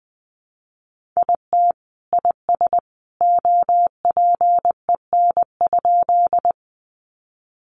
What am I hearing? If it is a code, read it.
Morse code, heard as a keyed tone, about 20 wpm, 700 Hz. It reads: IT IS OPEN?
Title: Sentences from Top 300 Words 20wpm